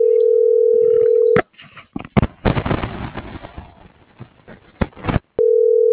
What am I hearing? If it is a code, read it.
Ringback tone (North America), 440 + 480 Hz.